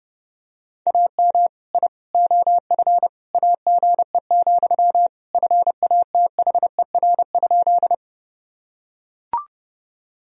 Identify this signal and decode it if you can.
Morse code, heard as a keyed tone, about 30 wpm, 700 Hz. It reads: AM I OF AGE, FATHER?
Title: Sentences from Top 1000 Words - Review Part 3 - 30wpm